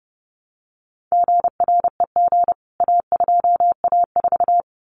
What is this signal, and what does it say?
Morse code, heard as a keyed tone, about 30 wpm, 700 Hz. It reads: GREG A2A4
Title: CWT Exchange 30wpm